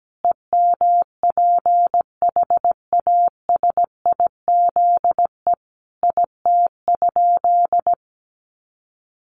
Morse code, heard as a keyed tone, about 17 wpm, 700 Hz. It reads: EMPHASIZE IT?
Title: Sentences from Top 1500 Words 17wpm